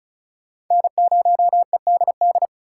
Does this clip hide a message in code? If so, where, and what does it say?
Yes: Morse code from 0.7 to 2.5 s, heard as a keyed tone, about 44 wpm, 700 Hz. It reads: N0EDD